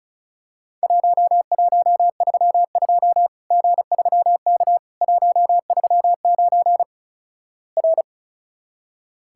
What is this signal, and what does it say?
Morse code, heard as a keyed tone, about 36 wpm, 700 Hz. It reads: 1132 G3K 139 R
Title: Worked All Europe DX Contest - QTC Exchange - 35wpm